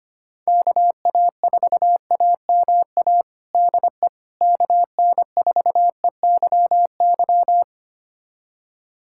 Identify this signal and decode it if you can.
Morse code, heard as a keyed tone, about 25 wpm, 700 Hz. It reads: KA4AMA DE KN4EYY
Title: USA Callsign DE Callsign 25wpm